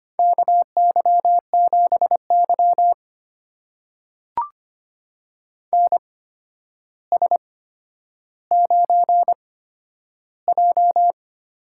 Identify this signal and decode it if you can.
Morse code, heard as a keyed tone, about 25 wpm, spaced as for 11 wpm, 700 Hz. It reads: KY7Y NS9J
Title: USA Call Signs - 4 Characters - 8x Character Spacing - 25wpm